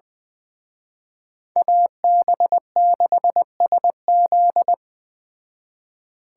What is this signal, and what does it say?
Morse code, heard as a keyed tone, about 20 wpm, 700 Hz. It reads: AB6SZ